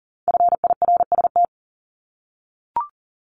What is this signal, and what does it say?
Morse code, heard as a keyed tone, about 39 wpm, 700 Hz. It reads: FIRST
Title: Multi-letter Phonemes 40wpm